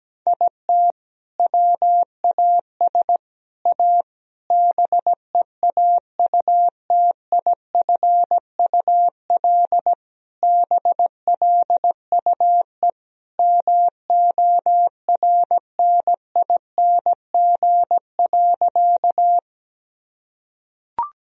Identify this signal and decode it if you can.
Morse code, heard as a keyed tone, about 17 wpm, 700 Hz. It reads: IT WAS A BEAUTIFUL BLUE MORNING.
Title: Sentences from Top 1000 Words - Review Part 3 - 17wpm